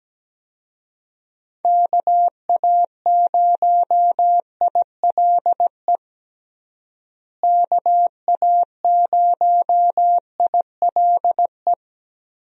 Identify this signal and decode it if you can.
Morse code, heard as a keyed tone, about 17 wpm, 700 Hz. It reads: KA0ILE KA0ILE
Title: USA Call Signs - 6 Characters - Repeated 2x - 17wpm